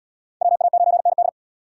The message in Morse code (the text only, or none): WA9ND